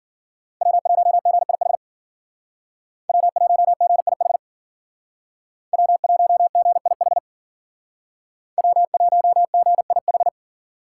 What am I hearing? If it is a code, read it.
Morse code, heard as a keyed tone, about 48 wpm, 700 Hz. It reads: W1GIH W1GIH W1GIH W1GIH
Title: Speed Racing - USA 1x3 Call Signs 40wpm